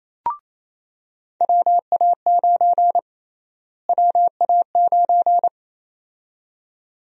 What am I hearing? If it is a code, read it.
Morse code, heard as a keyed tone, about 28 wpm, 700 Hz. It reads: WA9 WA9
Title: USA Call Sign Prefixes - Repeated 2x - 28wpm